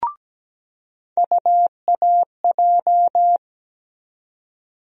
Morse code, heard as a keyed tone, about 17 wpm, 700 Hz. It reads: UAJ
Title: USA Call Sign Postfixes - 3 Letters 17wpm